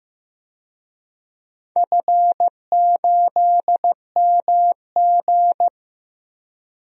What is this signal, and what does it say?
Morse code, heard as a keyed tone, about 15 wpm, 700 Hz. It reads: F8MG
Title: DXCC-100 Call Signs - 4 Characters - Repeated 2x - 15wpm